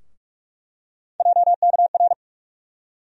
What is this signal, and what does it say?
Morse code, heard as a keyed tone, about 42 wpm, 700 Hz. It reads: JKR
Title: USA Call Sign Postfixes - 3 Letters 45wpm